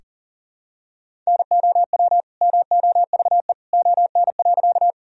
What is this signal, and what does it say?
Morse code, heard as a keyed tone, about 40 wpm, 700 Hz. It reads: NOW MOVE ON.
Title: Sentences from Top 1000 Words - Review Part 1 - 40wpm